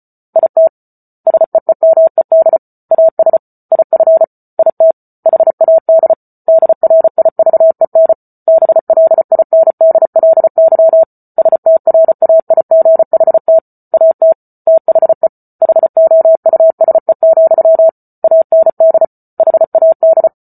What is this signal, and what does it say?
Morse code, heard as a keyed tone, about 34 wpm, 650 Hz. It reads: IT SEEMED AS IF IT HAD DRIVEN BLINDLY STRAIGHT AT THE HOUSE, AND HAD